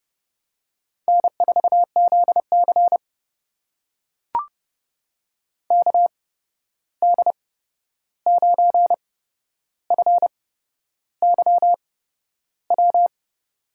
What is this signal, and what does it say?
Morse code, heard as a keyed tone, about 32 wpm, spaced as for 11 wpm, 700 Hz. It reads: N4ZC KD9FYW
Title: USA Call Signs - 8x Character Spacing - 30wpm